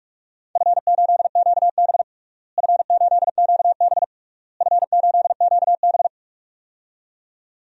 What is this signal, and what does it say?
Morse code, heard as a keyed tone, about 46 wpm, 700 Hz. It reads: F8QB F8QB F8QB
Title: DXCC-100 Call Signs - 4 Characters - Repeated 3x - 45wpm